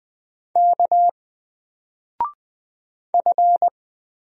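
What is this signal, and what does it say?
Morse code, heard as a keyed tone, about 20 wpm, 700 Hz. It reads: K F
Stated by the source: Single letter and number Morse drill at 20 wpm, from N4LCD.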